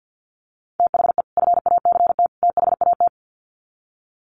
Morse code, heard as a keyed tone, about 50 wpm, 700 Hz. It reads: THE FACT THAT
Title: Sets of 3 Words 50wpm